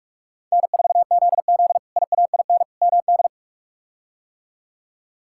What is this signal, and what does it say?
Morse code, heard as a keyed tone, about 46 wpm, 700 Hz. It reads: N4ZZ IAIN MD
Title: K1USN SST QSO Practice - 45wpm